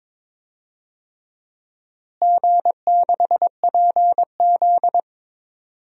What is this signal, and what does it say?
Morse code, heard as a keyed tone, about 22 wpm, 700 Hz. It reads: G6PZ
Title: DXCC-100 Call Signs - 4 Characters - Repeated 3x - 22wpm